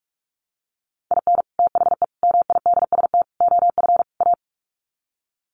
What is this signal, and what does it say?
Morse code, heard as a keyed tone, about 44 wpm, 700 Hz. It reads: IN THE MIDST OF A
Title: Sets of 5 Words 45wpm